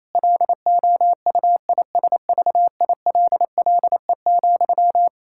Morse code, heard as a keyed tone, about 28 wpm, 700 Hz. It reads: LOUISVILLE,
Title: K1USN SST QSO Practice 28wpm